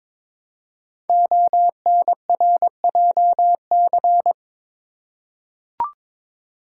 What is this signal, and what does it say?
Morse code, heard as a keyed tone, about 22 wpm, 700 Hz. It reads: ONRJC
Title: Code Groups - 5 Letters - 22wpm